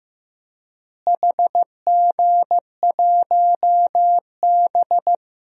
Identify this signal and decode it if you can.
Morse code, heard as a keyed tone, about 16 wpm, 700 Hz. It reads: HG1B
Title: DXCC-100 Call Signs - 4 Characters - Repeated 3x - 15wpm